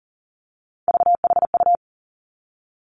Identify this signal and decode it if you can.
Morse code, heard as a keyed tone, about 41 wpm, 690 Hz. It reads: VHU
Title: Three Challenging Letter-Number Combinations 40wpm